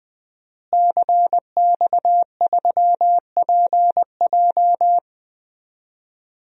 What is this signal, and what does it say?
Morse code, heard as a keyed tone, about 20 wpm, 700 Hz. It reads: CX3PJ
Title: DXCC-100 Call Signs - 5 Characters - Repeated 3x - 20wpm